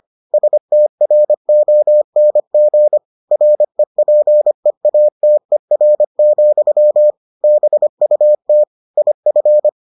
Morse code, heard as a keyed tone, about 25 wpm, 600 Hz. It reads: STRONG REPEATER, BUT IF